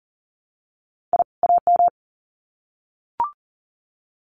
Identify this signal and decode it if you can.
Morse code, heard as a keyed tone, about 40 wpm, 700 Hz. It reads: I AM
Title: Sets of 2 Words 40wpm